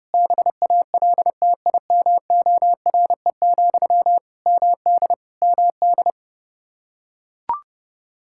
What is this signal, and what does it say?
Morse code, heard as a keyed tone, about 30 wpm, 700 Hz. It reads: BALTIMORE, MD MD
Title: Practice QSO - Exchange 1 - Beginning - 30wpm